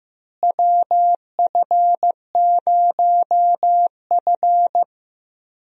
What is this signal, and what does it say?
Morse code, heard as a keyed tone, about 15 wpm, 700 Hz. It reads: WF0F
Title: USA Call Signs - 4 Characters - Repeated 2x - 15wpm